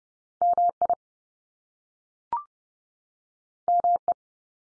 Morse code, heard as a keyed tone, about 30 wpm, 700 Hz. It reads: MI ME
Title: US States 30WPM